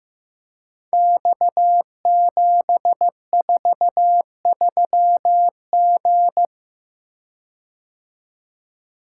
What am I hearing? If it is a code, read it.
Morse code, heard as a keyed tone, about 15 wpm, 700 Hz. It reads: X743G
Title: Code Groups - 5 Letter:Numbers - 15wpm